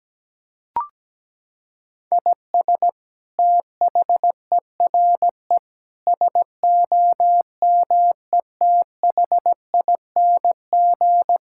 Morse code, heard as a keyed tone, about 17 wpm, 700 Hz. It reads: IS THERE SOMETHING
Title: Sentences from Top 1000 Words - Review Part 6 - 17wpm